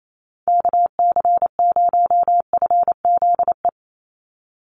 Morse code, heard as a keyed tone, about 28 wpm, 700 Hz. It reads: KC0FZE
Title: USA Call Signs 28wpm